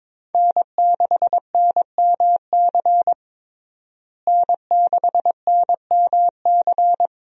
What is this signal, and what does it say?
Morse code, heard as a keyed tone, about 22 wpm, 700 Hz. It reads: N6NMC N6NMC
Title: USA Call Signs - 5 Characters - Repeated 2x - 22wpm